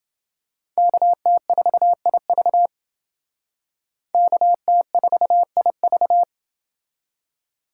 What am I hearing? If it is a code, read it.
Morse code, heard as a keyed tone, about 28 wpm, 700 Hz. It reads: KT4IV KT4IV